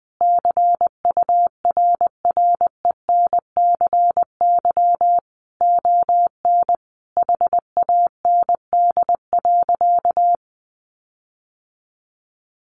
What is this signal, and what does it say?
Morse code, heard as a keyed tone, about 20 wpm, 700 Hz. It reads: CURRENCY ON HAND.
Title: Sentences from Top 1500 Words 20wpm